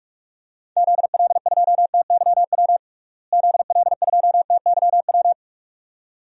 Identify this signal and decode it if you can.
Morse code, heard as a keyed tone, about 45 wpm, 700 Hz. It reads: ZL2TYW ZL2TYW
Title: DXCC-100 Call Signs - 6 Characters - Repeated 2x - 45wpm